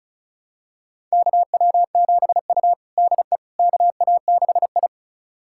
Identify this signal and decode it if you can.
Morse code, heard as a keyed tone, about 36 wpm, 700 Hz. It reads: KW7U DE KA6I
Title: USA Callsign DE Callsign - 4 Characters - 35wpm